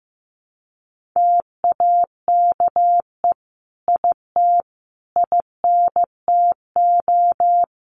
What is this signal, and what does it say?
Morse code, heard as a keyed tone, about 15 wpm, 700 Hz. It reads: TAKE IT INTO
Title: Sentences from Top 1500 Words 15wpm